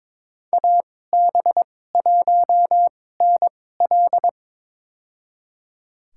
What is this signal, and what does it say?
Morse code, heard as a keyed tone, about 22 wpm, spaced as for 16 wpm, 700 Hz. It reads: AB1NL
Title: USA Call Signs - 2x Character Spacing - 22wpm